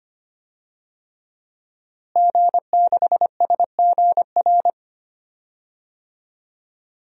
Morse code, heard as a keyed tone, about 25 wpm, 700 Hz. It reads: G6SGR
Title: DXCC-100 Call Signs - 5 Characters - Repeated 3x - 25wpm